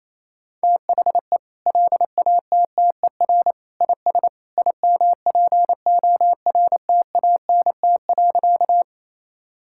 Morse code, heard as a keyed tone, about 28 wpm, 700 Hz. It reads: THE LATTER IS IMPORTANT.
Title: Sentences from Top 1600 Words 28wpm